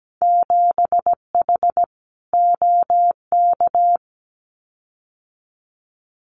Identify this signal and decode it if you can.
Morse code, heard as a keyed tone, about 17 wpm, 700 Hz. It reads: 7H OK
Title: Winter Field Day Exchanges Encore 17wpm